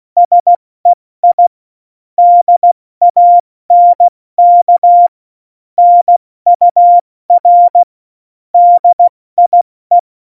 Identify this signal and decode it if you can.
Morse code, heard as a keyed tone, about 16 wpm, 700 Hz. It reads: SEI DANK NUR DIE
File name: Morsekurs / Witze / Witz 26. Februar 2019, 70 F80